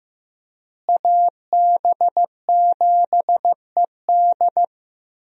Morse code, heard as a keyed tone, about 15 wpm, 700 Hz. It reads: AB7ED
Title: USA Call Signs - 5 Characters - Repeated 3x - 15wpm